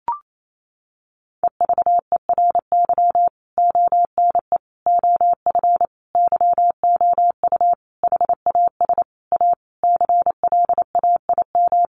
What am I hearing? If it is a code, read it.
Morse code, heard as a keyed tone, about 28 wpm, 700 Hz. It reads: EVERY ONE OF YOU HAS A CLAIM
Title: Sentences from Top 1500 Words 28wpm